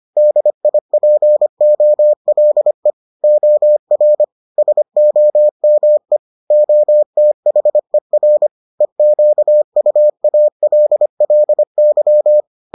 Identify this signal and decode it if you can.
Morse code, heard as a keyed tone, about 25 wpm, 600 Hz. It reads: DIPOLE OR SOME OTHER EQUALLY